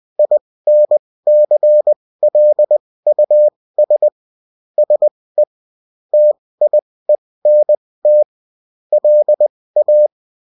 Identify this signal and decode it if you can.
Morse code, heard as a keyed tone, about 20 wpm, spaced as for 16 wpm, 600 Hz. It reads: INCLUS SE TIENT LA